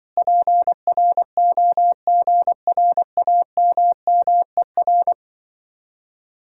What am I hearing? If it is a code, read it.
Morse code, heard as a keyed tone, about 24 wpm, 700 Hz. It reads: PROGRAMMER
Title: Speed-Racing Common QSO Phrases Encore - 20wpm